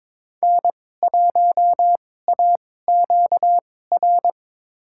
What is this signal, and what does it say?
Morse code, heard as a keyed tone, about 22 wpm, spaced as for 16 wpm, 700 Hz. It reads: N1AQR
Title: USA Call Signs - 5 Characters - 2x Character Spacing - 22wpm